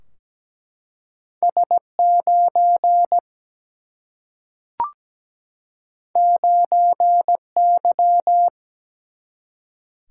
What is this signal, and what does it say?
Morse code, heard as a keyed tone, about 17 wpm, 700 Hz. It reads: S 9 9 Y
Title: Single-Letter Single-Number 17wpm